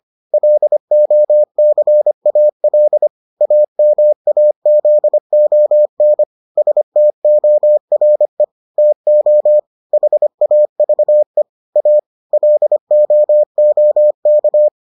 Morse code, heard as a keyed tone, about 25 wpm, 600 Hz. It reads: LOCAL AMAZON STORE TO HAVE A LOOK